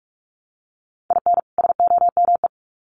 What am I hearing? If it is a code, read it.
Morse code, heard as a keyed tone, about 44 wpm, 700 Hz. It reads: IN SOME
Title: Sets of 2 Words 45wpm